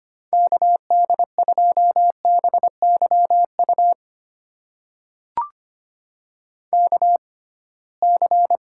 Morse code, heard as a keyed tone, about 25 wpm, spaced as for 19 wpm, 700 Hz. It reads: KD2BYU KC
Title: USA Call Signs - 6 Characters - 6x Character Spacing - 25wpm